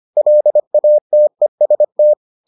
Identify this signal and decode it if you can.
Morse code, heard as a keyed tone, about 25 wpm, 600 Hz. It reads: LATEST